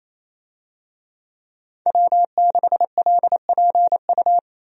Morse code, heard as a keyed tone, about 28 wpm, 700 Hz. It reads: W6LPU
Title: USA Call Signs - 5 Characters - 28wpm